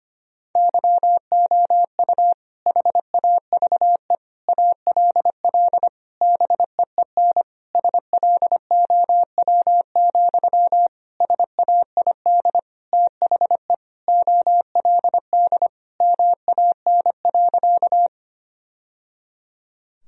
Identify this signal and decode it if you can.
Morse code, heard as a keyed tone, about 25 wpm, 700 Hz. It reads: YOU HAVE ALL BEEN SLOW, SAID THE OLD MAN.
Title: Sentences from Top 1000 Words - Review Part 2 - 25wpm